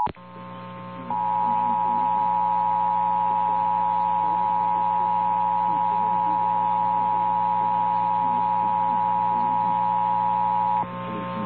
Emergency Alert System (SAME) tones (the 853 and 960 Hz attention tone), with noise. It is heard from 1.1 to 10.8 s.
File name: eas / KNCYKBIE / KNCYKBIE